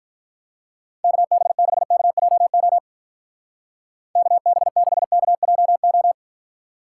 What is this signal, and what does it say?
Morse code, heard as a keyed tone, about 43 wpm, 700 Hz. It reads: KB6KJO KB6KJO
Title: Speed Racing - USA 2x3 Call Signs 35wpm